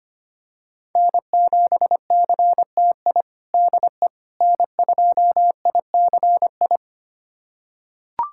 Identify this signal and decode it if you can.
Morse code, heard as a keyed tone, about 25 wpm, 700 Hz. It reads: N7CTI DE N2ICI
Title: USA Callsign DE Callsign - 5 Characters - 25wpm